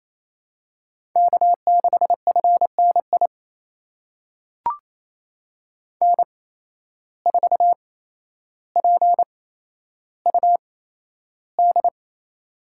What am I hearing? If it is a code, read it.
Morse code, heard as a keyed tone, about 28 wpm, spaced as for 12 wpm, 700 Hz. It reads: K6FNI N4PUD
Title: USA Call Signs - 5 Characters - 8x Character Spacing - 28wpm